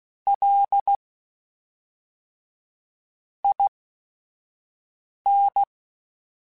Morse code, heard as a keyed tone, about 16 wpm, spaced as for 4 wpm, 800 Hz. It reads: LIN